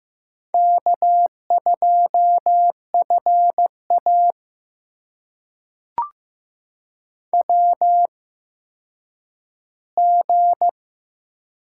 Morse code, heard as a keyed tone, about 15 wpm, spaced as for 9 wpm, 700 Hz. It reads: K2FA WG
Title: USA Call Signs - 4 Characters - 8x Character Spacing - 15wpm